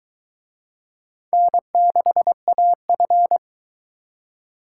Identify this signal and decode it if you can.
Morse code, heard as a keyed tone, about 23 wpm, 700 Hz. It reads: N6AF